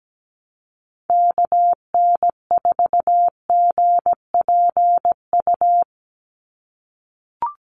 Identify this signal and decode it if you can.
Morse code, heard as a keyed tone, about 17 wpm, 700 Hz. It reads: KN4GPU
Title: USA Call Signs 17wpm